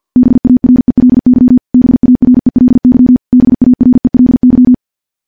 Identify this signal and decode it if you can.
Morse code, heard as a keyed tone, about 50 wpm, 250 Hz. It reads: BARELY BARELY BARELY
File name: morse / morse-training-50WPM